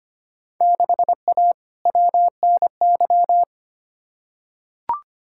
Morse code, heard as a keyed tone, about 25 wpm, 700 Hz. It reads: 6A WNY
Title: Field Day Exchanges - 25wpm